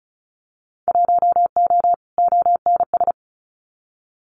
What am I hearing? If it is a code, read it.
Morse code, heard as a keyed tone, about 34 wpm, 700 Hz. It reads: 1O ONS
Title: Winter Field Day Exchanges 35wpm